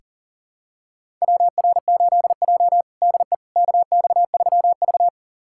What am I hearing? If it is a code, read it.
Morse code, heard as a keyed tone, about 40 wpm, 700 Hz. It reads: WR8J DE KX3V
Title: USA Callsign DE Callsign - 4 Characters - 40wpm